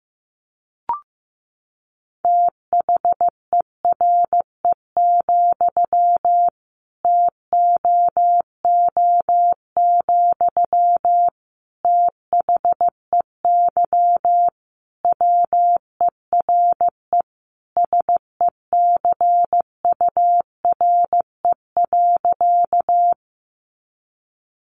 Morse code, heard as a keyed tone, about 15 wpm, 700 Hz. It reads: THERE, TOO, THEY WERE SECURE.
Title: Sentences from Top 1200 Words 15wpm